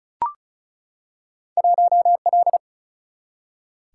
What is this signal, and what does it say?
Morse code, heard as a keyed tone, about 36 wpm, 700 Hz. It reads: 1L
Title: Single-Letter Single-Number 35wpm